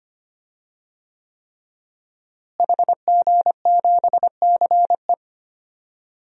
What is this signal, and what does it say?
Morse code, heard as a keyed tone, about 25 wpm, 700 Hz. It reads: HG7CE
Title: DXCC-100 Call Signs - 5 Characters - Repeated 3x - 25wpm